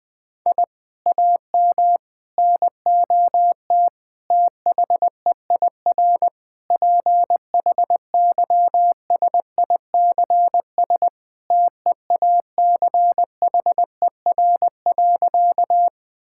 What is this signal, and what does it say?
Morse code, heard as a keyed tone, about 21 wpm, 700 Hz. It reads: I AM NOT THEIR PHYSICS TEACHER.
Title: Sentences from Top 1000 Words - Review Part 6 - 20wpm